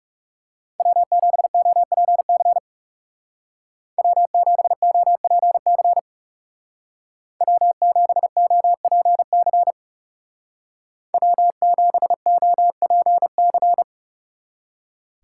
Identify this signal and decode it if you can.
Morse code, heard as a keyed tone, about 39 wpm, 700 Hz. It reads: W7OPC W7OPC W7OPC W7OPC